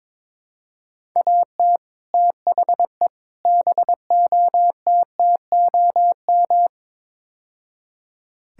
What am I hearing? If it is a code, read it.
Morse code, heard as a keyed tone, about 22 wpm, 700 Hz. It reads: AT THE BOTTOM
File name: practice / 22wpm / Build A Phrase - 5-Word Phrases - 22wpm